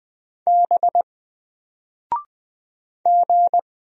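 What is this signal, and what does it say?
Morse code, heard as a keyed tone, about 20 wpm, 700 Hz. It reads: B G
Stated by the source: Single letter and number Morse drill at 20 wpm, from N4LCD.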